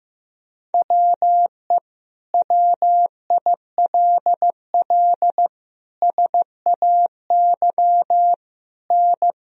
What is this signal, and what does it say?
Morse code, heard as a keyed tone, about 15 wpm, 700 Hz. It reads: WE WILL SAY N
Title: Sentences from Top 200 Words and Abbreviations 15wpm